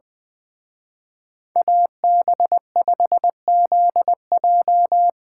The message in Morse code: AB5ZJ

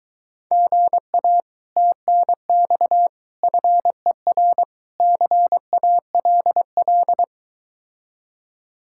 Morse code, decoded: GA TNX FER CALL

